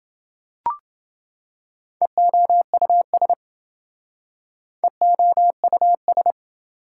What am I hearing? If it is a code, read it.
Morse code, heard as a keyed tone, about 27 wpm, 700 Hz. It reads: EOUS EOUS